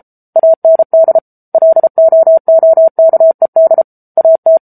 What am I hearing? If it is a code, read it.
Morse code, heard as a keyed tone, about 34 wpm, 650 Hz. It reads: AND LOOKED AT
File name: Book Two / 07-Chapter 7 - 33 WPM